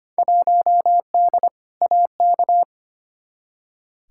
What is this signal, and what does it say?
Morse code, heard as a keyed tone, about 25 wpm, 700 Hz. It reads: 1D AK
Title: Field Day Exchanges - 25wpm